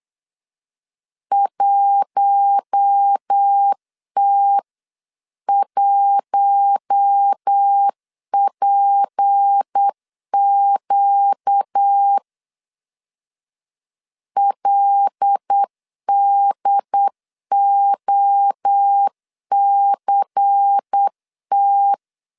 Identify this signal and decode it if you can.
Morse code, heard as a keyed tone, about 8 wpm, 800 Hz. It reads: 1T1PQ LDOCT
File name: CW / DET / 1 6 1500C S B 8wpm 5142